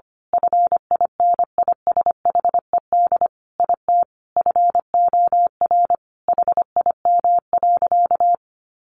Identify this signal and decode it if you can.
Morse code, heard as a keyed tone, about 25 wpm, 700 Hz. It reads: FINISHED IT FOR HIM.
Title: A Sound Of Thunder 25wpm